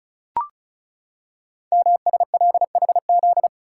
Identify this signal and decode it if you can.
Morse code, heard as a keyed tone, about 36 wpm, 700 Hz. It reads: MSLHZ
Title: Code Groups - 5 Letter:Numbers - 35wpm